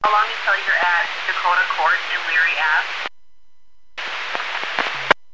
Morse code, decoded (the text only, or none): none